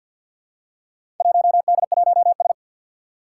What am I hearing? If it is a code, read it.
Morse code, heard as a keyed tone, about 53 wpm, 700 Hz. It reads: 1D1S
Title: Four Challenging Letter-Number Combinations 50wpm